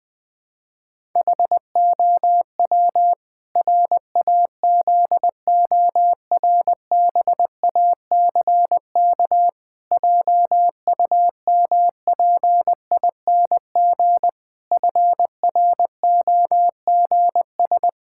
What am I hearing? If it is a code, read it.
Morse code, heard as a keyed tone, about 19 wpm, 700 Hz. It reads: HOW RAZORBACK JUMPING FROGS
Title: Pangram Challenge v2 - 20wpm